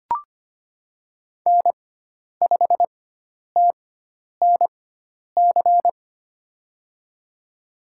Morse code, heard as a keyed tone, about 25 wpm, spaced as for 10 wpm, 700 Hz. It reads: N5TNC
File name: Lesson 16g - Letter c - Call Signs 25wpm